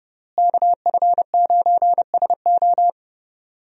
Morse code, heard as a keyed tone, about 30 wpm, 700 Hz. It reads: KF9SO
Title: USA Call Signs - 5 Characters - Repeated 3x - 30wpm